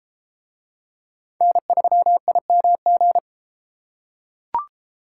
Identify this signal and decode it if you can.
Morse code, heard as a keyed tone, about 32 wpm, 700 Hz. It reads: N3IMG